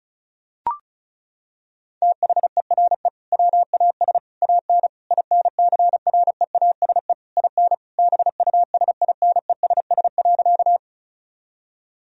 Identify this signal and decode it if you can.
Morse code, heard as a keyed tone, about 36 wpm, 700 Hz. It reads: THERE WAS AN INCREASE IN BUSINESS.
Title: Sentences from Top 1000 Words - Review Part 4 - 35wpm